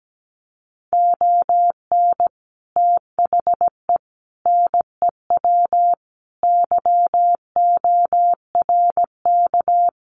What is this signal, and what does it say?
Morse code, heard as a keyed tone, about 17 wpm, 700 Hz. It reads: ON THE NEW YORK
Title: Sets of 5 Words Encore 17wpm